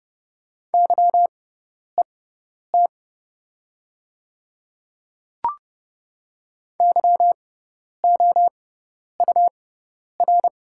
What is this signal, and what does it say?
Morse code, heard as a keyed tone, about 31 wpm, spaced as for 10 wpm, 700 Hz. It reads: YET YOUR
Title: 6x Character Spacing - Top 500 Words - 30wpm